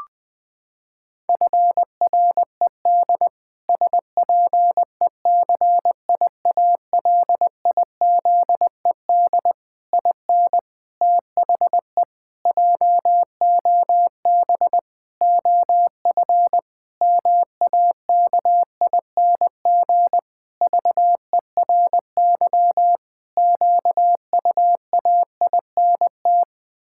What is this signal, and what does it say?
Morse code, heard as a keyed tone, about 20 wpm, 700 Hz. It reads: FRED SPECIALIZED IN THE JOB OF MAKING VERY QUAINT
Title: Pangram Challenge v2 - 20wpm